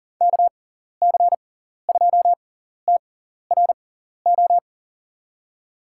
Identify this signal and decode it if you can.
Morse code, heard as a keyed tone, about 39 wpm, spaced as for 14 wpm, 700 Hz. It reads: KC2TRO
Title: USA Call Signs - 6 Characters - 6x Character Spacing - 40wpm